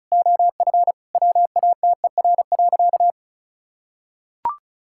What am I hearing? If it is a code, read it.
Morse code, heard as a keyed tone, about 36 wpm, 700 Hz. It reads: OF WATER.
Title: Sentences from Top 1000 Words - Review Part 5 - 35wpm